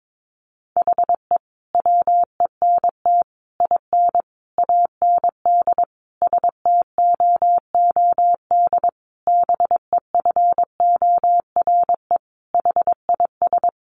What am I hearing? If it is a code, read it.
Morse code, heard as a keyed tone, about 22 wpm, 700 Hz. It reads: HE WENT IN AND STOOD BEFORE HIS
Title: Sentences from Top 1300 Words 22wpm